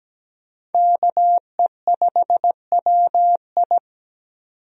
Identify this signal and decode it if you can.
Morse code, heard as a keyed tone, about 17 wpm, 700 Hz. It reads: KE5WI
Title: USA Call Signs - 5 Characters - 17wpm